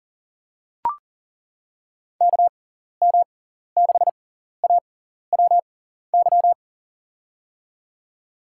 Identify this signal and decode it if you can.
Morse code, heard as a keyed tone, about 40 wpm, spaced as for 14 wpm, 690 Hz. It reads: KM6AWY
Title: USA Call Signs - 6x Character Spacing - 40wpm